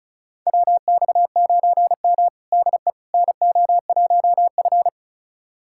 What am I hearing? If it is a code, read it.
Morse code, heard as a keyed tone, about 34 wpm, 700 Hz. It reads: WX9M DE NO1F
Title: USA Callsign DE Callsign - 4 Characters - 35wpm